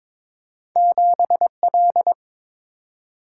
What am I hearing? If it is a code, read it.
Morse code, heard as a keyed tone, about 22 wpm, 700 Hz. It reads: 7 L
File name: practice / 22wpm / Two Letter-Number Combinations - 22wpm